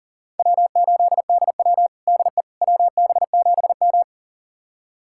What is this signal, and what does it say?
Morse code, heard as a keyed tone, about 40 wpm, 700 Hz. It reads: W8DW DE WB7M
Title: USA Callsign DE Callsign - 4 Characters - 40wpm